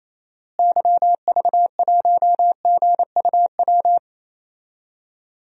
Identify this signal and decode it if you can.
Morse code, heard as a keyed tone, about 28 wpm, 700 Hz. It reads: YV1GUW
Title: DXCC-100 Call Signs - 6 Characters - Repeated 3x - 28wpm